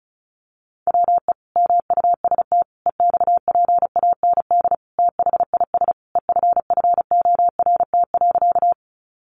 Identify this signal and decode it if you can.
Morse code, heard as a keyed tone, about 34 wpm, 700 Hz. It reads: WE MUST EXPAND THIS EFFORT.
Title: Sentences from Top 1400 Words 35wpm